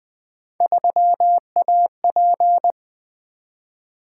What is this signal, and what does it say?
Morse code, heard as a keyed tone, about 20 wpm, 700 Hz. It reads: 3AP
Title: Three Letter-Number Combinations - 20wpm